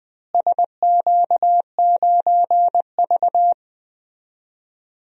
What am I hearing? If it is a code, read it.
Morse code, heard as a keyed tone, about 21 wpm, 700 Hz. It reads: SQ9V
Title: DXCC-100 Call Signs - 4 Characters - Repeated 2x - 20wpm